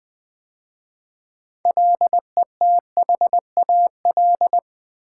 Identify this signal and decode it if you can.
Morse code, heard as a keyed tone, about 20 wpm, 700 Hz. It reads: LETHAL
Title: 6 Letter Words - First 3 Letters Given - 20wpm